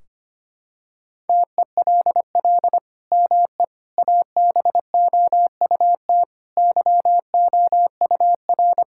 Morse code, heard as a keyed tone, about 25 wpm, 700 Hz. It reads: TELL ME ABOUT YOUR
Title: Sentences from Top 1600 Words 25wpm